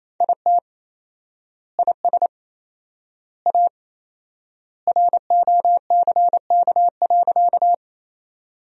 Morse code, heard as a keyed tone, about 27 wpm, 700 Hz. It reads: IT IS A ROCK.
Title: 4x Word-Spacing - Sentences from Top 400 Words - 28wpm